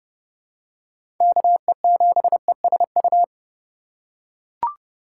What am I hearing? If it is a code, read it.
Morse code, heard as a keyed tone, about 30 wpm, 700 Hz. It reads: KE7ESU
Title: USA Call Signs - 6 Characters - 2x Character Spacing - 30wpm